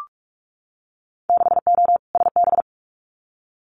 Morse code, heard as a keyed tone, about 46 wpm, 700 Hz. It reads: 6O SB
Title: Winter Field Day Exchanges Encore 45wpm